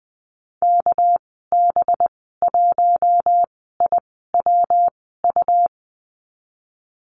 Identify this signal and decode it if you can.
Morse code, heard as a keyed tone, about 20 wpm, spaced as for 15 wpm, 700 Hz. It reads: KB1IWU
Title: 238 Lesson 38g - Letter q - Call Signs 20wpm